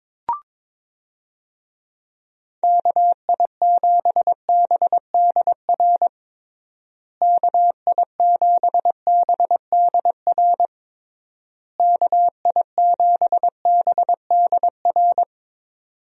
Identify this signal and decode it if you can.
Morse code, heard as a keyed tone, about 22 wpm, 700 Hz. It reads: KI7BDR KI7BDR KI7BDR
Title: USA Call Signs - 6 Characters - Repeated 3x - 22wpm